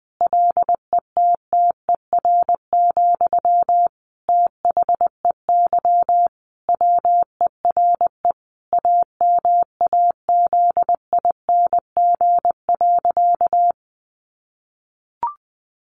Morse code, heard as a keyed tone, about 19 wpm, 700 Hz. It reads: LETTER, THEY WERE AMAZING.
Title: Sentences from Top 1500 Words 20wpm